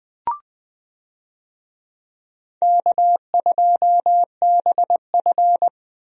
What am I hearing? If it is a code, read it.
Morse code, heard as a keyed tone, about 20 wpm, 700 Hz. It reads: K2BF
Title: USA Call Signs - 4 Characters - 20wpm